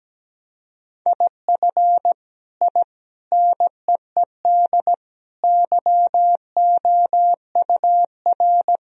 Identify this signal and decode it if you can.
Morse code, heard as a keyed tone, about 17 wpm, 700 Hz. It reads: IF I NEED YOUR
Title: Sentences from Top 1000 Words - Review Part 1 - 17wpm